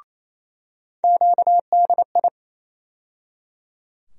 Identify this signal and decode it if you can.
Morse code, heard as a keyed tone, about 28 wpm, 700 Hz. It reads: QDI
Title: USA Call Sign Postfixes - 3 Letters 28wpm